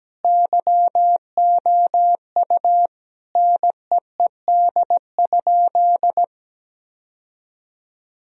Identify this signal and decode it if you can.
Morse code, heard as a keyed tone, about 17 wpm, 700 Hz. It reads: YOU NEED?
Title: Sentences from Top 1000 Words - Review Part 2 - 17wpm